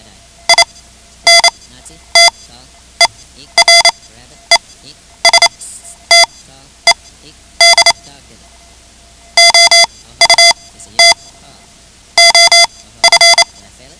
Morse code, heard as a keyed tone, about 28 wpm, spaced as for 12 wpm, 2050 Hz. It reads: INTERESTED OUT OF
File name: ham radio / CodeQuick / Code Quick 10